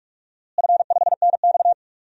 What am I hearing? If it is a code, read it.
Morse code, heard as a keyed tone, about 46 wpm, 700 Hz. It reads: F5NX
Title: DXCC-100 Call Signs - 4 Characters - Repeated 3x - 45wpm